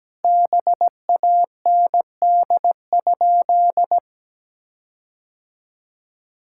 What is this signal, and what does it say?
Morse code, heard as a keyed tone, about 17 wpm, 700 Hz. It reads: BAND?
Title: Sentences from Top 1600 Words 17wpm